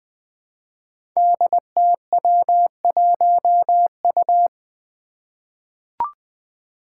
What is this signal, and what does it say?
Morse code, heard as a keyed tone, about 20 wpm, 700 Hz. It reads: DTW1U
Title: Five Challenging Letter-Number Combinations 20wpm